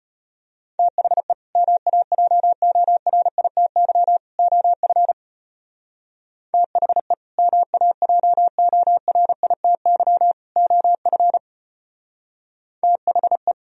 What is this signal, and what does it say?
Morse code, heard as a keyed tone, about 35 wpm, 700 Hz. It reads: THE MAJORITY OF THE MAJORITY OF THE